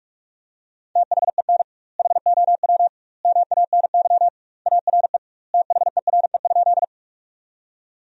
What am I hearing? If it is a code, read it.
Morse code, heard as a keyed tone, about 41 wpm, 700 Hz. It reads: THEN HOW MANY ARE THERE?
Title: 2x Word-Spacing - Sentences from Top 100 Words - 45wpm